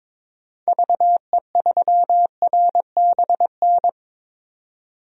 Morse code, heard as a keyed tone, about 22 wpm, 700 Hz. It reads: VE3RBN
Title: DXCC-100 Call Signs - 6 Characters - 22wpm